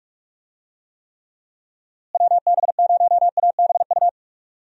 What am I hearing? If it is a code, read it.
Morse code, heard as a keyed tone, about 46 wpm, 700 Hz. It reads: WB0ABU
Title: USA Call Signs - 6 Characters - Repeated 3x - 45wpm